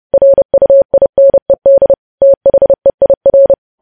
Morse code, heard as a keyed tone, about 30 wpm, 550 Hz. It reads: RUINED THEIR
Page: news headlines in morse code at 30 wpm